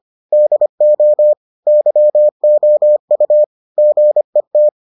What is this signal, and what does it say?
Morse code, heard as a keyed tone, about 25 wpm, 600 Hz. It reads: DO YOU GET